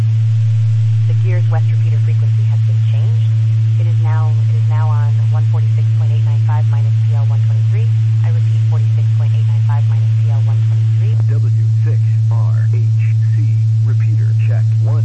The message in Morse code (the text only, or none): none